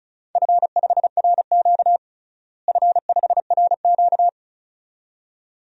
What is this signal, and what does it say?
Morse code, heard as a keyed tone, about 36 wpm, 700 Hz. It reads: F5RQ F5RQ
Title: DXCC-100 Call Signs - 4 Characters - Repeated 2x - 35wpm